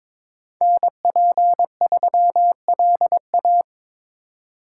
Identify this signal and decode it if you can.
Morse code, heard as a keyed tone, about 25 wpm, 700 Hz. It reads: NP3LA
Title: Speed Racing - USA Call Signs 22wpm